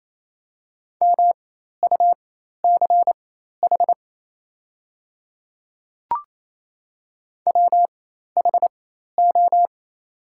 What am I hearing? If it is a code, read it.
Morse code, heard as a keyed tone, about 27 wpm, spaced as for 13 wpm, 700 Hz. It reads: MUCH WHO
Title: 4x Character Spacing - Top 300 Words - 28wpm